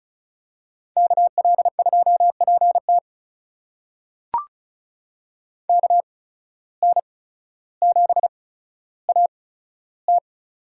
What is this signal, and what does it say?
Morse code, heard as a keyed tone, about 36 wpm, spaced as for 15 wpm, 700 Hz. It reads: KL2PT KN7AT
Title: USA Call Signs - 5 Characters - 8x Character Spacing - 35wpm